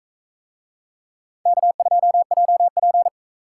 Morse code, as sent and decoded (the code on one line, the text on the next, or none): -.- ..--- .--- .--.
K2JP